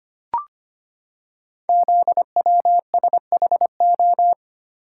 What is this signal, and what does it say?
Morse code, heard as a keyed tone, about 25 wpm, 700 Hz. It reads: ZWSHO
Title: Code Groups - 5 Letters - 25wpm